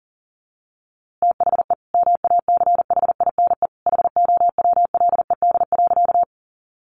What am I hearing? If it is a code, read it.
Morse code, heard as a keyed tone, about 40 wpm, 700 Hz. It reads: THE MACHINE HOWLED.
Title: A Sound Of Thunder 40wpm